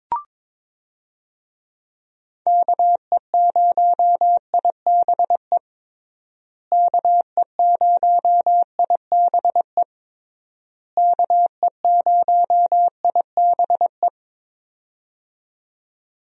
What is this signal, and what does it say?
Morse code, heard as a keyed tone, about 22 wpm, 700 Hz. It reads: KE0IBE KE0IBE KE0IBE
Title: USA Call Signs - 6 Characters - Repeated 3x - 22wpm